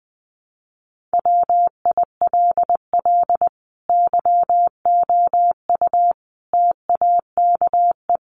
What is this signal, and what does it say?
Morse code, heard as a keyed tone, about 19 wpm, 700 Hz. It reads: WILL YOU TAKE
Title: Sentences from Top 900 Words 20wpm